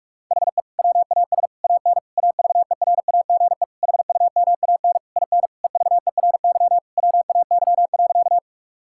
Morse code, heard as a keyed tone, about 46 wpm, 700 Hz. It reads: HE WAS AN AVERAGE HUMAN IN EVERY WAY.
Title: Sentences from Top 1000 Words - Review Part 6 - 45wpm